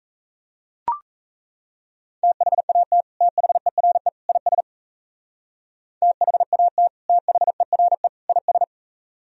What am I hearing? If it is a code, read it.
Morse code, heard as a keyed tone, about 35 wpm, 700 Hz. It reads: THAT THERE IS THAT THERE IS
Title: Speed-Racing - Sets of 3 Words 28wpm